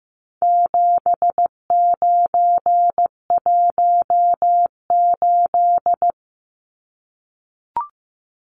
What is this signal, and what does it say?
Morse code, heard as a keyed tone, about 15 wpm, 700 Hz. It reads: 7918
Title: Numbers - 4-Digit 15wpm